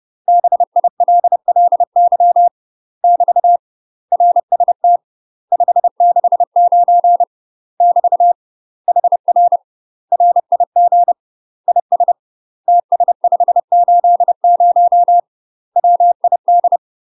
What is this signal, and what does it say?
Morse code, heard as a keyed tone, about 30 wpm, 700 Hz. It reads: BILLY <BT> RST 569 <BT> HR RIG IS TS580 WID
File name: UK MASTER QSOs TRAINING / 30WPM / UK MASTER QSOs 0009